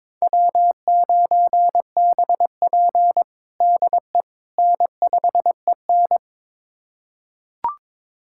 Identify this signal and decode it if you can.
Morse code, heard as a keyed tone, about 22 wpm, 700 Hz. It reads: W9BP DE N5EN
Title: USA Callsign DE Callsign - 4 Characters - 22wpm